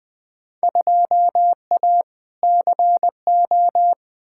Morse code, heard as a keyed tone, about 20 wpm, 700 Hz. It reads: 2A CO
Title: Field Day Exchanges - 20wpm